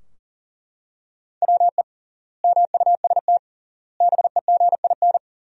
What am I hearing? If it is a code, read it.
Morse code, heard as a keyed tone, about 40 wpm, 700 Hz. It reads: WE MUST BEGIN
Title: 3x Word-Spacing - Sentences from Top 300 Words - 40wpm